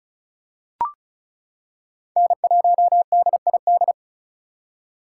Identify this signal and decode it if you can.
Morse code, heard as a keyed tone, about 36 wpm, 700 Hz. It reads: N1DID
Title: K1USN SST QSO Practice - 35wpm